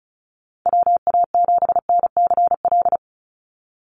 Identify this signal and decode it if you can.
Morse code, heard as a keyed tone, about 34 wpm, 700 Hz. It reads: WA7NCL
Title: NA SOTA Activator Callsigns 35wpm